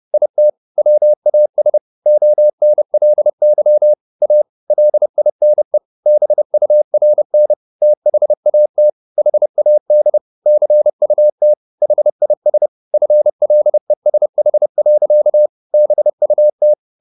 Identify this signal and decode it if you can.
Morse code, heard as a keyed tone, about 30 wpm, 600 Hz. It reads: IT WAS ONLY A LINE BURN THAT HAD CUT HIS FLESH. BUT
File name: the old man and the sea / the old man and the sea 1 30wpm